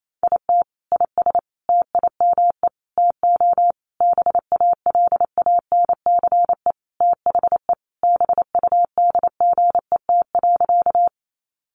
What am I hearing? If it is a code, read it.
Morse code, heard as a keyed tone, about 28 wpm, 700 Hz. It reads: IT IS TIME TO BALANCE THE BUDGET.
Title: Sentences from Top 1200 Words 28wpm